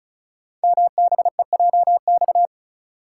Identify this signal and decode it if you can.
Morse code, heard as a keyed tone, about 36 wpm, 690 Hz. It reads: MBEJX
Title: Code Groups - 5 Letters - 35wpm